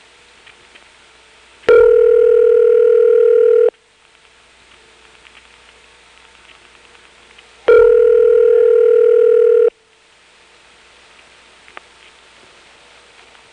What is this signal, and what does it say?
Ringback tone (North America), 440 + 480 Hz.
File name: phonetrips / SmyrnaGA5HQ